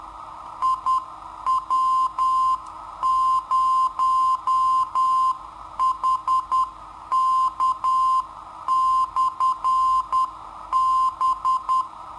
Morse code, heard as a keyed tone, about 10 wpm, 1040 Hz. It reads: IW0HK/B
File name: sitoariold / beacon hk